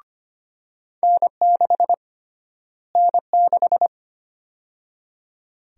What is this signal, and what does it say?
Morse code, heard as a keyed tone, about 25 wpm, 700 Hz. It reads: N6 N6
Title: USA Call Sign Prefixes - Repeated 2x - 25wpm